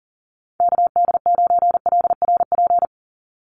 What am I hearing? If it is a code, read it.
Morse code, heard as a keyed tone, about 40 wpm, 700 Hz. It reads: KD9LRP
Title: USA Call Signs 40wpm